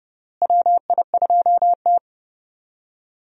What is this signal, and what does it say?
Morse code, heard as a keyed tone, about 32 wpm, 700 Hz. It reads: WI2T
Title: USA Call Signs - 4 Characters - Repeated 3x - 30wpm